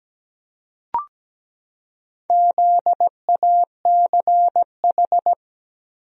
Morse code, heard as a keyed tone, about 17 wpm, 700 Hz. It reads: ZACH
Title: USA Common Names - Max 4 Letters - 17wpm